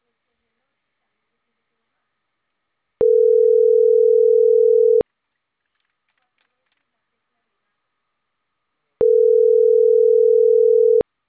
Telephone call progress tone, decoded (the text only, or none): ringback tone